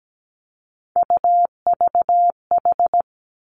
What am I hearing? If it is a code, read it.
Morse code, heard as a keyed tone, about 17 wpm, 700 Hz. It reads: UVH
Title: Three Challenging Letter-Number Combinations 17wpm